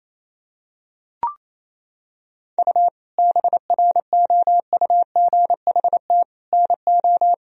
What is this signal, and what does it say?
Morse code, heard as a keyed tone, about 28 wpm, 700 Hz. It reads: U BROUGHT NO
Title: Sentences from Top 500 Words and Abbreviations 28wpm